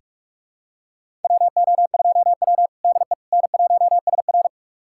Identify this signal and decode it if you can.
Morse code, heard as a keyed tone, about 46 wpm, 700 Hz. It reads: WO2W DE N1SR
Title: USA Callsign DE Callsign - 4 Characters - 45wpm